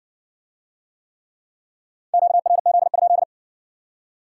Morse code, heard as a keyed tone, about 56 wpm, 700 Hz. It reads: QRZ?